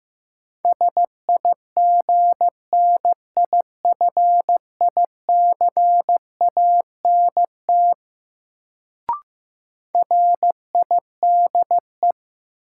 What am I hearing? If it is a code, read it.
Morse code, heard as a keyed tone, about 16 wpm, 700 Hz. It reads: SIGNIFICANT RIDE
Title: Sentences from Top 1000 Words - Review Part 6 - New Word Drill - 15wpm